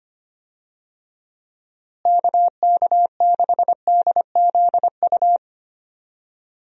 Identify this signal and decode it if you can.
Morse code, heard as a keyed tone, about 25 wpm, 700 Hz. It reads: KK6DZU
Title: USA Call Signs - 6 Characters - Repeated 3x - 25wpm